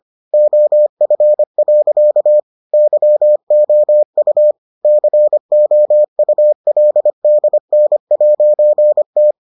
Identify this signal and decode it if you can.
Morse code, heard as a keyed tone, about 25 wpm, 600 Hz. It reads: OF. YOU COULDN'T